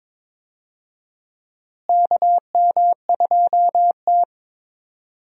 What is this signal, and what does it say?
Morse code, heard as a keyed tone, about 22 wpm, 700 Hz. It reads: KM2T
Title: USA Call Signs - 4 Characters - Repeated 2x - 22wpm